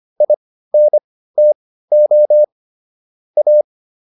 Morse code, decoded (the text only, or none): INTO A